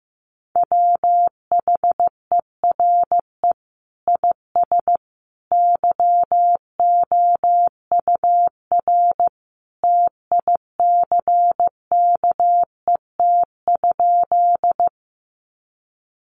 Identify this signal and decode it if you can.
Morse code, heard as a keyed tone, about 15 wpm, 700 Hz. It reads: WHERE IS YOUR TICKET?
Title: Sentences from Top 1400 Words 15wpm